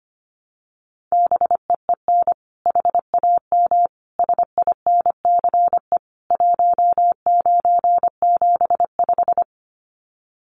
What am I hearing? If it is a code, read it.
Morse code, heard as a keyed tone, about 25 wpm, 700 Hz. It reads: BEEN HAM SINCE 1975
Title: Common QSO Phrases 25wpm